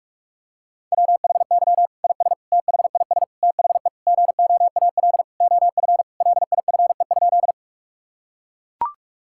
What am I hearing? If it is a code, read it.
Morse code, heard as a keyed tone, about 41 wpm, 700 Hz. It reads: WHY IS THIS THE GOAL OF LIFE?
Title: Sentences from Top 1000 Words - Review Part 5 - 45wpm